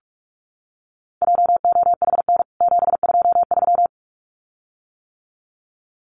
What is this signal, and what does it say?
Morse code, heard as a keyed tone, about 44 wpm, 700 Hz. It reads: JOHN 723
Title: CWT Exchange 45wpm